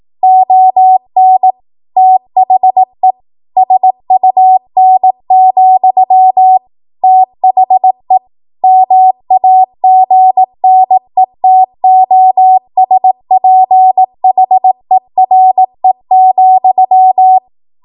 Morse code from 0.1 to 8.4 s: ON THE SUN, THE